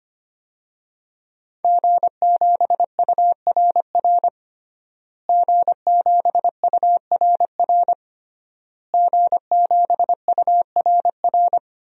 Morse code, heard as a keyed tone, about 25 wpm, 700 Hz. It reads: G7URR G7URR G7URR
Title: DXCC-100 Call Signs - 5 Characters - Repeated 3x - 25wpm